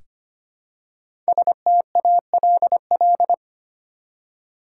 Morse code, heard as a keyed tone, about 25 wpm, 700 Hz. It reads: STALL
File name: 5 Letter Words - First 3 Letters Given - 25wpm